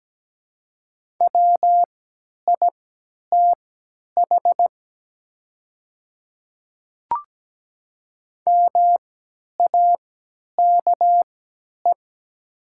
Morse code, heard as a keyed tone, about 17 wpm, spaced as for 10 wpm, 700 Hz. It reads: WITH MAKE
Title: 3x Character Spacing - Top 100 Words - 17wpm